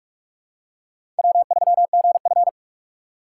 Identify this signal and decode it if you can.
Morse code, heard as a keyed tone, about 46 wpm, 700 Hz. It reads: W3GF
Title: USA Call Signs - 4 Characters - 3x Character Spacing - 45wpm